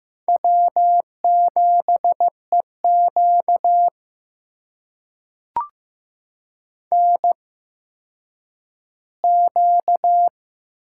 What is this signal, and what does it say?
Morse code, heard as a keyed tone, about 15 wpm, spaced as for 9 wpm, 700 Hz. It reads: W7EQ NQ
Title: USA Call Signs - 4 Characters - 8x Character Spacing - 15wpm